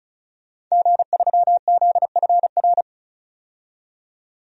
Morse code, heard as a keyed tone, about 36 wpm, 700 Hz. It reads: G3ZFR
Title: DXCC-100 Call Signs - 5 Characters - 35wpm